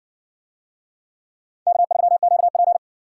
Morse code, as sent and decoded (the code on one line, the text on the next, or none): -.- ...-- --.- .--.
K3QP